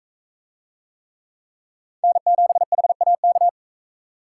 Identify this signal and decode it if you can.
Morse code, heard as a keyed tone, about 39 wpm, 700 Hz. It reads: N7HAK